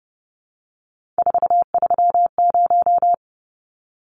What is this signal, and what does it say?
Morse code, heard as a keyed tone, about 30 wpm, 700 Hz. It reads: 430
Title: Numbers - 3-Digit 30wpm